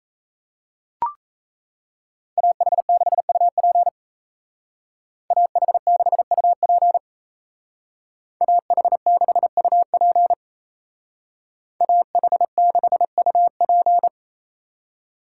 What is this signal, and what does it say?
Morse code, heard as a keyed tone, about 35 wpm, 700 Hz. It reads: AH6UP AH6UP AH6UP AH6UP
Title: Speed Racing - USA 2x2 Call Signs 28wpm